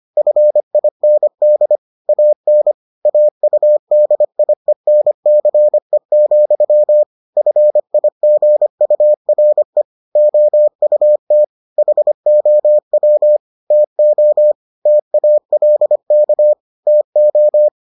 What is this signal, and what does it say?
Morse code, heard as a keyed tone, about 25 wpm, 600 Hz. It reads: FIND AN AUDIENCE, FIGURE OUT HOW TO TALK TO